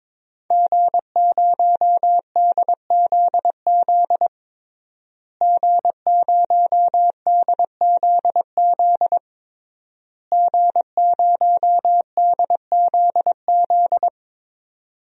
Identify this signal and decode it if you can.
Morse code, heard as a keyed tone, about 22 wpm, 700 Hz. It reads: G0DZZ G0DZZ G0DZZ
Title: DXCC-100 Call Signs - 5 Characters - Repeated 3x - 22wpm